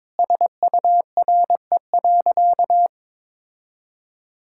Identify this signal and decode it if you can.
Morse code, heard as a keyed tone, about 22 wpm, 700 Hz. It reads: SURE.
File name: practice / 22wpm / Sentences with 2-Word Phrases - 22wpm